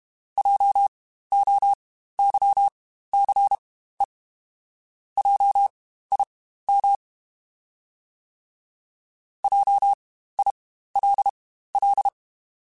Morse code, heard as a keyed tone, about 32 wpm, spaced as for 15 wpm, 800 Hz. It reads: JOYCE JIM JILL
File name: Spaced Alphas / Spaced Alphas mp3s / Js sp @30